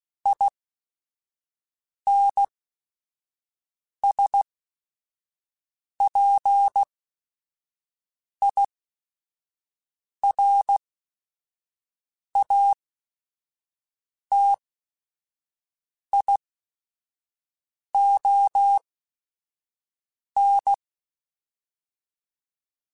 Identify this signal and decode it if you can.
Morse code, heard as a keyed tone, about 16 wpm, spaced as for 5 wpm, 800 Hz. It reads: INSPIRATION
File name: Quotes / Quotes-mp3s / Persperation @5